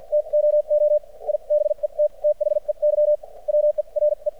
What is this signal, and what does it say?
Morse code, heard as a keyed tone, about 53 wpm, spaced as for 42 wpm, 600 Hz. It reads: TOO I BET THEY WERE
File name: archived / CRSnet